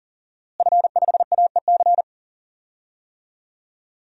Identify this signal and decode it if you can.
Morse code, heard as a keyed tone, about 40 wpm, 700 Hz. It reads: F5AEC
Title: DXCC-100 Call Signs - 5 Characters - Repeated 3x - 40wpm